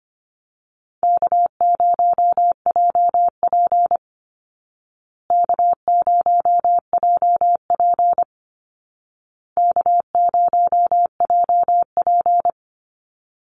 Morse code, heard as a keyed tone, about 25 wpm, 700 Hz. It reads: K0JP K0JP K0JP
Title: Call Signs Repeated 2x - 25wpm